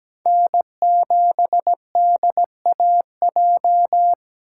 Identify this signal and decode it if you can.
Morse code, heard as a keyed tone, about 20 wpm, 700 Hz. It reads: N7DAJ
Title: Speed Racing - USA Call Signs 17wpm